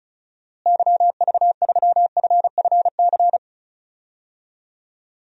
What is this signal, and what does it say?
Morse code, heard as a keyed tone, about 36 wpm, 700 Hz. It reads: YV3FFC
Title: DXCC-100 Call Signs - 6 Characters - Repeated 3x - 35wpm